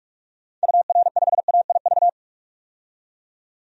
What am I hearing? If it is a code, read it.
Morse code, heard as a keyed tone, about 46 wpm, 700 Hz. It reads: UR5AIV
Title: DXCC-100 Call Signs - 6 Characters - 45wpm